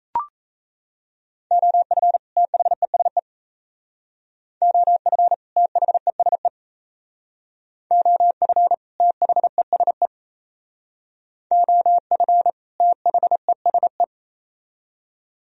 Morse code, heard as a keyed tone, about 37 wpm, 700 Hz. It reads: OF THESE OF THESE OF THESE OF THESE